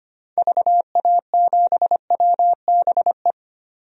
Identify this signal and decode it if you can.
Morse code, heard as a keyed tone, about 25 wpm, 700 Hz. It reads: VA7WBE
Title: DXCC-100 Call Signs - 6 Characters - 25wpm